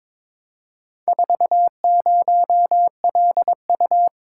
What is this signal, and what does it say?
Morse code, heard as a keyed tone, about 22 wpm, 700 Hz. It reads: 40LU
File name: practice / 22wpm / Four Letter-Number Combinations - 22wpm